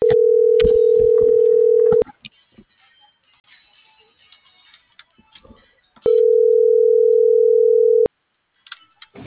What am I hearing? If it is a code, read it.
Ringback tone (North America), 440 + 480 Hz.